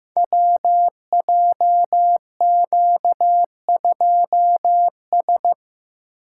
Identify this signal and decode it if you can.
Morse code, heard as a keyed tone, about 15 wpm, 700 Hz. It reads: WJQ2S
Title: Code Groups - 5 Letter:Numbers - 15wpm